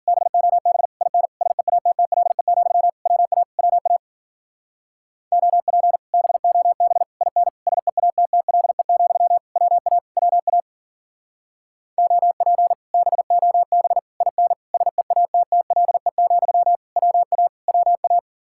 Morse code, heard as a keyed tone, about 50 wpm, 700 Hz. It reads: BOB IN SEATTLE, WA WA OP BOB IN SEATTLE, WA WA OP BOB IN SEATTLE, WA WA